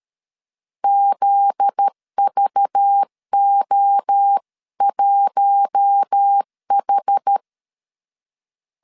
Morse code, heard as a keyed tone, about 13 wpm, 800 Hz. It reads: ZVO1H
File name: CW / DET / 1 9 1500C S C 12wpm 3811